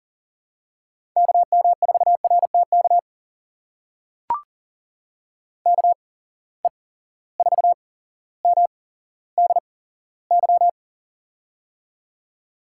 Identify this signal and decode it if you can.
Morse code, heard as a keyed tone, about 43 wpm, spaced as for 17 wpm, 700 Hz. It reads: KM4RTK KE4MDY
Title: USA Call Signs - 6 Characters - 8x Character Spacing - 40wpm